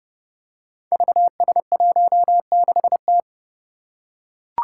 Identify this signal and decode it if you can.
Morse code, heard as a keyed tone, about 30 wpm, 700 Hz. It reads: VS16T
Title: Five Challenging Letter-Number Combinations 30wpm